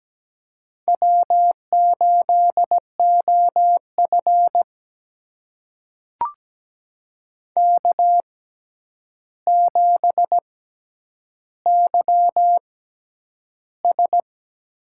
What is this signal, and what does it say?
Morse code, heard as a keyed tone, about 17 wpm, spaced as for 9 wpm, 700 Hz. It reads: W8OF K7YS
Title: USA Call Signs - 4 Characters - 6x Character Spacing - 17wpm